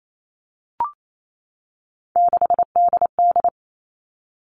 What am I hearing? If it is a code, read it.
Morse code, heard as a keyed tone, about 28 wpm, 700 Hz. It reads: 6DD
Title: All Cut Numbers - 3-Digit 28wpm